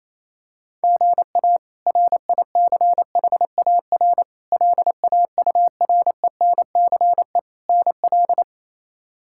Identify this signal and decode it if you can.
Morse code, heard as a keyed tone, about 28 wpm, 700 Hz. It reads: GA RICHAR LAURENCE NL
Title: K1USN SST QSO Practice - 28wpm